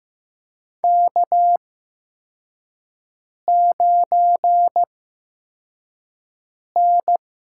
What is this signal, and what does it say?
Morse code, heard as a keyed tone, about 16 wpm, spaced as for 4 wpm, 700 Hz. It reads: K9N
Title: USA Call Signs - 8x Character Spacing - 15wpm